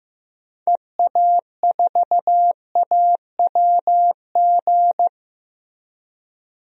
Morse code, heard as a keyed tone, about 15 wpm, 700 Hz. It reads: EA4AWG
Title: DXCC-100 Call Signs - 6 Characters - Repeated 3x - 15wpm